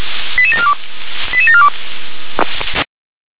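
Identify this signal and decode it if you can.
Five-tone selective call (ZVEI), heard as quick runs of five tones, tones of about 70 ms. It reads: 88742 88742
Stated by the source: ZVEI five-tone selective call received on SDR.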